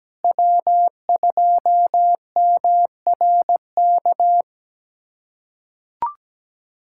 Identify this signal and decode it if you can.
Morse code, heard as a keyed tone, about 17 wpm, 700 Hz. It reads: W2MRK
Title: USA Callsign DE Callsign 17wpm